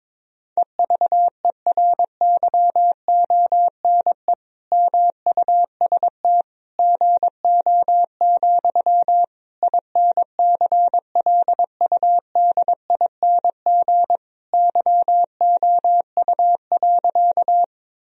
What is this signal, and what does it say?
Morse code, heard as a keyed tone, about 22 wpm, 700 Hz. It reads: EVERYONE MUST GO, INCLUDING YOU.
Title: Sentences from Top 1000 Words - Review Part 4 - 22wpm